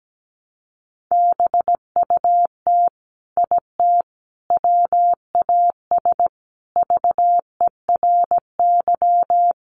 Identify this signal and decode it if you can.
Morse code, heard as a keyed tone, about 17 wpm, 700 Hz. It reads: BUT IT WAS VERY
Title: Sentences from Top 300 Words 17wpm